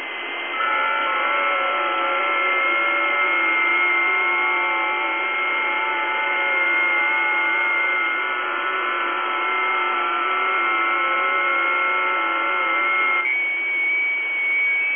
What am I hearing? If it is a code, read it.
FT8.
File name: tests / websdr test10